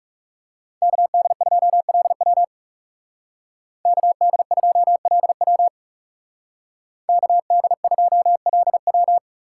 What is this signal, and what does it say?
Morse code, heard as a keyed tone, about 35 wpm, 700 Hz. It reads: KD2LW KD2LW KD2LW
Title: Speed Racing - USA Call Signs 30wpm